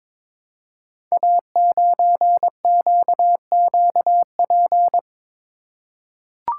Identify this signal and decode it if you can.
Morse code, heard as a keyed tone, about 22 wpm, 700 Hz. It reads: A9QQP
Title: Code Groups - 5 Letter:Numbers - 22wpm